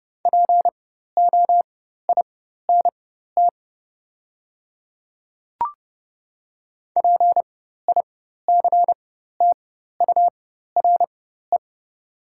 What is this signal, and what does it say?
Morse code, heard as a keyed tone, about 30 wpm, spaced as for 14 wpm, 700 Hz. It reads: POINT PICTURE
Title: 4x Character Spacing - Top 300 Words - 30wpm